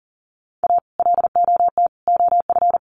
Morse code, heard as a keyed tone, about 40 wpm, 700 Hz. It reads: A LOT OF
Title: Sets of 3 Words 40wpm